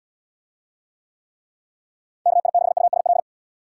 Morse code, heard as a keyed tone, about 74 wpm, 700 Hz. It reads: KI6USL